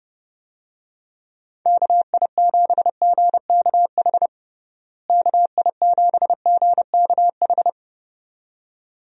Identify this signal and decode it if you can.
Morse code, heard as a keyed tone, about 32 wpm, 700 Hz. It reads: KI7GKH KI7GKH
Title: USA Call Signs - 6 Characters - Repeated 2x - 30wpm